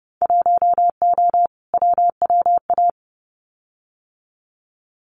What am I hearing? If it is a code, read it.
Morse code, heard as a keyed tone, about 30 wpm, 700 Hz. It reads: 1O WWA
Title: Winter Field Day Exchanges 30wpm